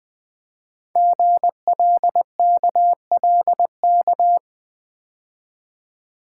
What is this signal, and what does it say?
Morse code, heard as a keyed tone, about 20 wpm, 700 Hz. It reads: GLKLK
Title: Code Groups - 5 Letters - 20wpm